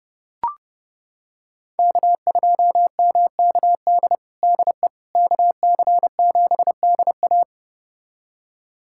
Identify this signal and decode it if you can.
Morse code, heard as a keyed tone, about 30 wpm, 700 Hz. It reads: K2MKD DE KC7DA
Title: USA Callsign DE Callsign - 5 Characters - 30wpm